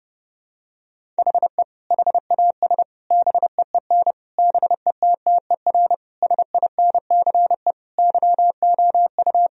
Morse code, heard as a keyed tone, about 32 wpm, 700 Hz. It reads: HE HAS BEEN BETTER SINCE YOU
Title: Sentences from Top 1000 Words - Review Part 1 - 30wpm